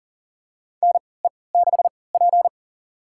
Morse code, heard as a keyed tone, about 39 wpm, spaced as for 23 wpm, 700 Hz. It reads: NE6P
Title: USA Call Signs - 3x Character Spacing - 40wpm